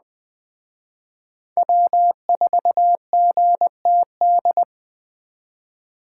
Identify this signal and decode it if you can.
Morse code, heard as a keyed tone, about 20 wpm, 700 Hz. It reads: W4GTD
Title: Speed Racing - USA 1x3 Call Signs 17wpm